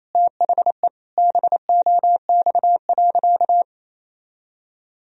Morse code, heard as a keyed tone, about 28 wpm, 700 Hz. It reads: THE BOX.
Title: Sentences from Top 1000 Words - Review Part 4 - 28wpm